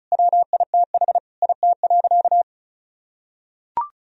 Morse code, heard as a keyed tone, about 34 wpm, 700 Hz. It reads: WITH IT.
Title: Sentences from Top 1000 Words - Review Part 1 - 35wpm